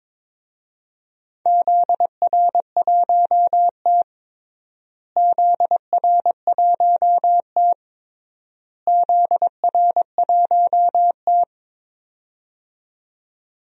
Morse code, heard as a keyed tone, about 22 wpm, 700 Hz. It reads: ZR1T ZR1T ZR1T
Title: DXCC-100 Call Signs - 4 Characters - Repeated 3x - 22wpm